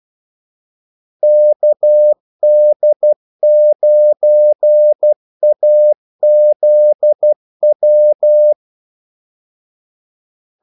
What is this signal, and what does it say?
Morse code, heard as a keyed tone, about 12 wpm, 600 Hz. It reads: KD9AZW
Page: 2 ekstra ordmellemrum mellem kaldesignalerne